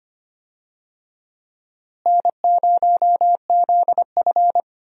Morse code, heard as a keyed tone, about 25 wpm, 700 Hz. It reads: N0ZF
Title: USA Call Signs - 4 Characters - Repeated 3x - 25wpm